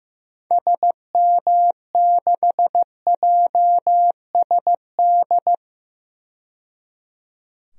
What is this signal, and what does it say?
Morse code, heard as a keyed tone, about 15 wpm, 700 Hz. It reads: SM6JSD
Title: DXCC-100 Call Signs - 6 Characters - 15wpm